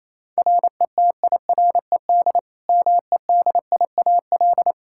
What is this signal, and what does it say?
Morse code, heard as a keyed tone, about 27 wpm, 700 Hz. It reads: RETIRED MEDIAL
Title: Practice QSO - Exchange 3 - More Info - 28wpm